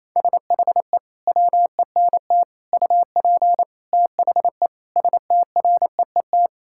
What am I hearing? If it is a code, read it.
Morse code, heard as a keyed tone, about 27 wpm, 700 Hz. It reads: SHE WENT UP THE STREET
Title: Sentences from Top 1000 Words - Review Part 5 - 28wpm